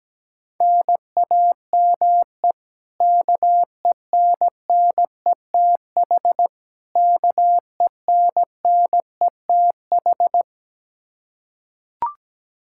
Morse code, heard as a keyed tone, about 17 wpm, 700 Hz. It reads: NAME KENNETH KENNETH
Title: Practice QSO - Exchange 1 - Beginning - 17wpm